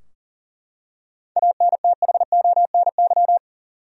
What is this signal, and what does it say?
Morse code, heard as a keyed tone, about 40 wpm, 700 Hz. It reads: ANTHONY
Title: USA Common Names - 40wpm